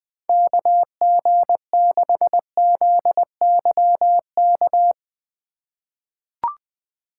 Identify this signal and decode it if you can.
Morse code, heard as a keyed tone, about 20 wpm, 700 Hz. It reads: KG6ZYK
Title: USA Call Signs - 3x Character Spacing - 20wpm